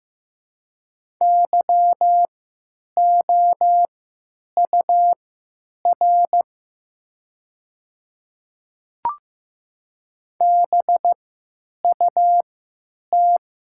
Morse code, heard as a keyed tone, about 15 wpm, spaced as for 9 wpm, 700 Hz. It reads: YOUR BUT
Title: 3x Character Spacing - Top 200 Words - 15wpm